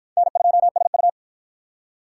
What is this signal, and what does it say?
Morse code, heard as a keyed tone, about 53 wpm, 700 Hz. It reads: N2SU